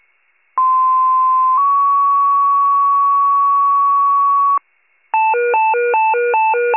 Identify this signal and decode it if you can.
Two-tone page, heard as a one-second tone, then a three-second tone, about 1040 then 1120 Hz. It is heard from 0.6 to 4.6 s.